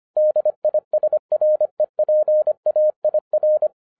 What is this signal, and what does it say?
Morse code, heard as a keyed tone, about 25 wpm, 610 Hz. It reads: DISREPAIR
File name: FIRST FISTS CONTACT-25wpm